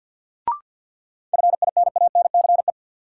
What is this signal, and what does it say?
Morse code, heard as a keyed tone, about 52 wpm, 700 Hz. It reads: FINANCE
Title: Sentences from Top 1500 Words - New Word Drill Alt - 50wpm